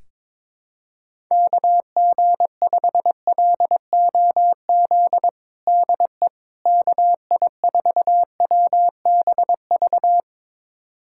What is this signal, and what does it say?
Morse code, heard as a keyed tone, about 22 wpm, 700 Hz. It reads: KG5LOZ DE KI4WBV
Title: USA Callsign DE Callsign - 6 Characters - 22wpm